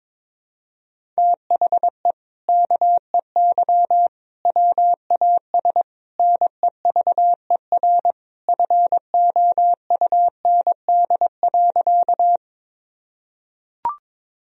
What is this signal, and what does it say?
Morse code, heard as a keyed tone, about 21 wpm, 700 Hz. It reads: THE KEY WAS NEVER FOUND.
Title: Sentences from Top 1000 Words - Review Part 4 - 22wpm